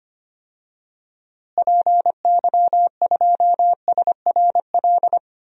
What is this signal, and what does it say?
Morse code, heard as a keyed tone, about 25 wpm, 700 Hz. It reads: PY2SRL
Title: DXCC-100 Call Signs - 6 Characters - Repeated 2x - 25wpm